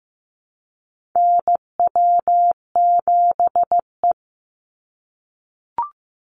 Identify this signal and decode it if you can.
Morse code, heard as a keyed tone, about 15 wpm, 700 Hz. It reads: NW7E
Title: NA SOTA Chaser Callsigns 15wpm